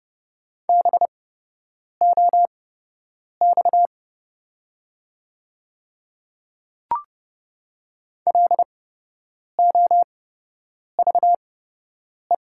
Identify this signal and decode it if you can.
Morse code, heard as a keyed tone, about 30 wpm, spaced as for 8 wpm, 700 Hz. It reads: BOX LOVE
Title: 8x Character Spacing - Top 500 Words - 30wpm